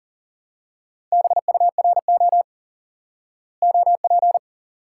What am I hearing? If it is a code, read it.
Morse code, heard as a keyed tone, about 40 wpm, 700 Hz. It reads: BURO OP
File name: IQE Course 3 - Lesson 89 - YAGI - Two Elements - 40wpm